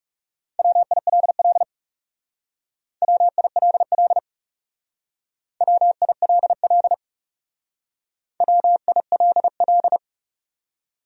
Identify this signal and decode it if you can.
Morse code, heard as a keyed tone, about 38 wpm, 700 Hz. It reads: WILL WILL WILL WILL